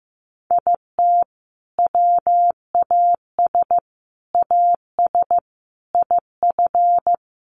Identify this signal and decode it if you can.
Morse code, heard as a keyed tone, about 15 wpm, 700 Hz. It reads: IT WAS AS IF
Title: Sets of 4 Words 15wpm